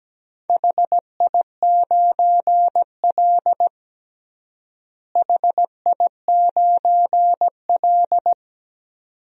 Morse code, heard as a keyed tone, about 17 wpm, 700 Hz. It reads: HI9L HI9L
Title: DXCC-100 Call Signs - 4 Characters - Repeated 2x - 17wpm